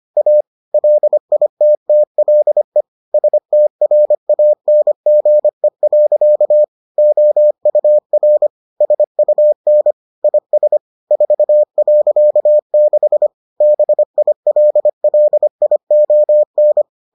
Morse code, heard as a keyed tone, about 25 wpm, 600 Hz. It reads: A LITTLE STRANGE. OUR SUN IS 4.6 BILLION